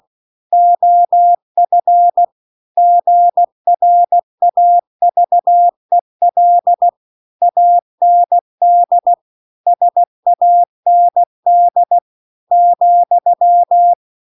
Morse code, heard as a keyed tone, about 16 wpm, 700 Hz. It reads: OF GRAVEL AND SAND ,
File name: CW-The-Big-Twoheaded-River-1-16WPM